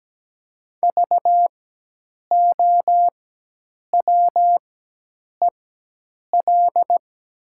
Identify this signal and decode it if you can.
Morse code, heard as a keyed tone, about 17 wpm, spaced as for 8 wpm, 700 Hz. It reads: VOWEL
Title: 4x Character Spacing - Top 500 Words - 17wpm